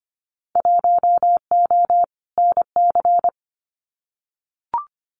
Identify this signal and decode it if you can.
Morse code, heard as a keyed tone, about 25 wpm, 700 Hz. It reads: 1O NC
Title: Winter Field Day Exchanges 25wpm